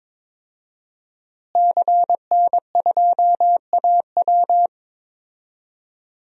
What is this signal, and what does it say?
Morse code, heard as a keyed tone, about 22 wpm, 700 Hz. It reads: CN2AW
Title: DXCC-100 Call Signs - 5 Characters - Repeated 2x - 22wpm